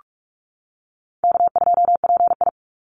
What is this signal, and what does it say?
Morse code, heard as a keyed tone, about 44 wpm, 700 Hz. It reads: K2PI
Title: Easy Call Signs - 45wpm